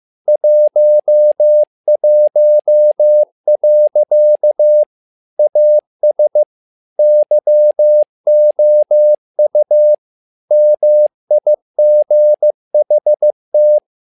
Morse code, heard as a keyed tone, about 15 wpm, 600 Hz. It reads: 11. AS YOU MIGHT
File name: learning center / tower trouble / tower trouble 15wpm